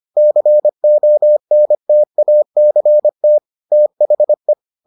Morse code, heard as a keyed tone, about 25 wpm, 600 Hz. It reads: CONTACT THE